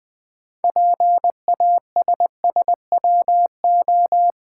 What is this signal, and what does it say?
Morse code, heard as a keyed tone, about 20 wpm, 700 Hz. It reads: PASSWO
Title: Build A Word - 3 Letters At A Time - 8-Letter Words - 20wpm